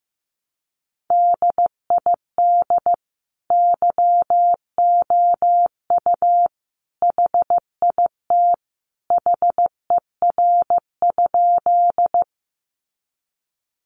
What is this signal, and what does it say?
Morse code, heard as a keyed tone, about 15 wpm, 700 Hz. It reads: DID YOU HIT HER?
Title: Sentences from Top 1200 Words 15wpm